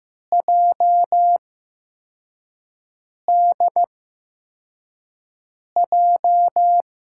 Morse code, heard as a keyed tone, about 15 wpm, spaced as for 4 wpm, 700 Hz. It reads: JDJ
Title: USA Call Signs - 6 Characters - 8x Character Spacing - 15wpm